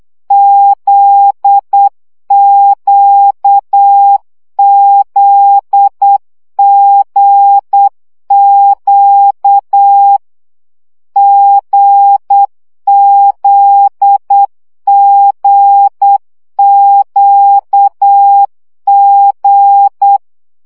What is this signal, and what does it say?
Morse code, heard as a keyed tone, about 8 wpm, 800 Hz. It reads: ZQZGQ GZGQG
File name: CW gyakorlas g z q 7 baud